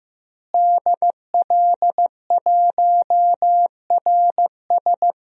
Morse code, heard as a keyed tone, about 15 wpm, 700 Hz. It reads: DL1RS
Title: DXCC-100 Call Signs - 5 Characters - Repeated 3x - 15wpm